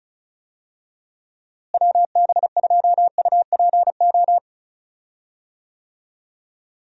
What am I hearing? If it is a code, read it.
Morse code, heard as a keyed tone, about 36 wpm, 700 Hz. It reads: WB2UPO
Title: USA Call Signs - 6 Characters - Repeated 3x - 35wpm